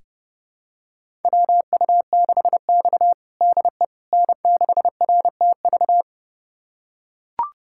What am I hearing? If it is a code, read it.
Morse code, heard as a keyed tone, about 31 wpm, 700 Hz. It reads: WU6X DE N6RTV
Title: USA Callsign DE Callsign 30wpm